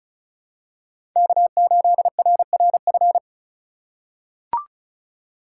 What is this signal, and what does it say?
Morse code, heard as a keyed tone, about 36 wpm, 690 Hz. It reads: K8RRF
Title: USA Call Signs - 5 Characters - 3x Character Spacing - 35wpm